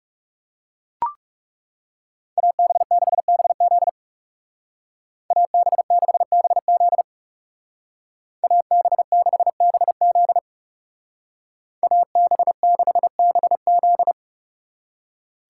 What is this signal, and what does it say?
Morse code, heard as a keyed tone, about 36 wpm, 700 Hz. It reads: AB6BZ AB6BZ AB6BZ AB6BZ
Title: Speed Racing - USA 2x2 Call Signs 30wpm